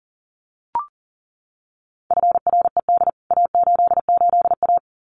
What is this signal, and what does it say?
Morse code, heard as a keyed tone, about 40 wpm, 700 Hz. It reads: FRED A88A
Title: CWT Exchange 40wpm